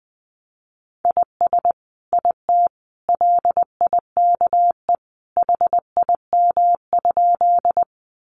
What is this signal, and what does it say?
Morse code, heard as a keyed tone, about 20 wpm, 700 Hz. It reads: IS IT LIKE HIM?
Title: Sentences from Top 100 Words 20wpm